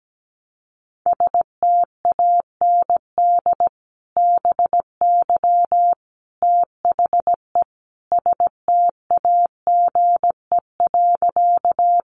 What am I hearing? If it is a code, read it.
Morse code, heard as a keyed tone, about 17 wpm, 700 Hz. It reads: STAND BY THE STAGE.
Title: Sentences from Top 1200 Words 17wpm